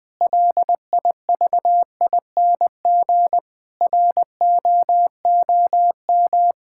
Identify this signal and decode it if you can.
Morse code, heard as a keyed tone, about 20 wpm, 700 Hz. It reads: LIVING ROOM
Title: Build A Phrase - 4-Word Phrases - 20wpm